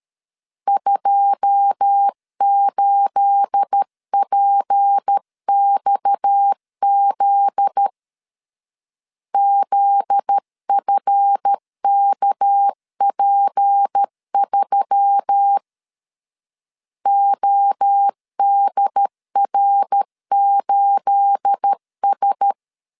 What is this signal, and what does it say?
Morse code, heard as a keyed tone, about 13 wpm, 800 Hz. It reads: 28PXZ ZFKP3 ODR8S
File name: LEZIONI / 9 28 12wpm 2351